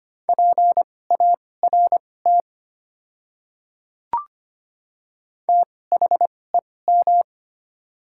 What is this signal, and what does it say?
Morse code, heard as a keyed tone, about 25 wpm, spaced as for 18 wpm, 700 Hz. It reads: PART THEM
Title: 2x Character Spacing - Top 200 Words - 25wpm